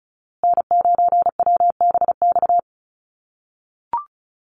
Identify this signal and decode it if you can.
Morse code, heard as a keyed tone, about 34 wpm, 700 Hz. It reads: N9WBX
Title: USA Call Signs Repeated 2x 35wpm